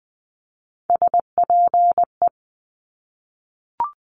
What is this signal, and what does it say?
Morse code, heard as a keyed tone, about 20 wpm, 700 Hz. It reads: SPE
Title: Top Three Letter Combinations in Words 20wpm